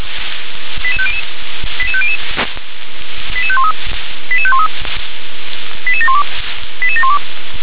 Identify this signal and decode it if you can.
Five-tone selective call (ZVEI), heard as quick runs of five tones, tones of about 70 ms. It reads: 88500 88500 88513 88513 88712 88712